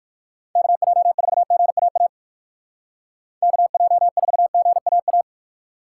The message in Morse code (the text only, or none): KJ4GAA KJ4GAA